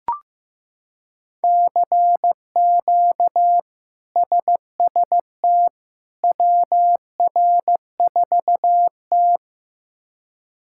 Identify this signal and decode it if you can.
Morse code, heard as a keyed tone, about 15 wpm, 700 Hz. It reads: CQ SST WR4T
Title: K1USN SST QSO Practice - 15wpm